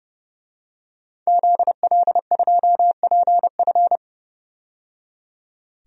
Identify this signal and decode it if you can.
Morse code, heard as a keyed tone, about 31 wpm, 700 Hz. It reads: ZL2PF
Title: DXCC-100 Call Signs - 5 Characters - 30wpm